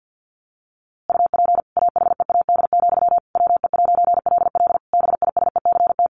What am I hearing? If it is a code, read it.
Morse code, heard as a keyed tone, about 50 wpm, 700 Hz. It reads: UP AHEAD, WE'LL BISECT
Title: A Sound Of Thunder 50wpm